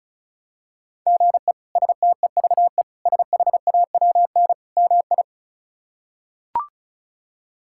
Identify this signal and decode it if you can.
Morse code, heard as a keyed tone, about 36 wpm, 700 Hz. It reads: GE STEVE SHAWN MI
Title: K1USN SST QSO Practice - 35wpm